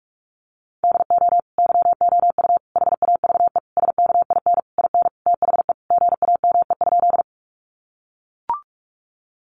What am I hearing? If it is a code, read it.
Morse code, heard as a keyed tone, about 44 wpm, 700 Hz. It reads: DO YOU HAVE SKIN IN THE GAME?
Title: Sentences from Top 1300 Words 45wpm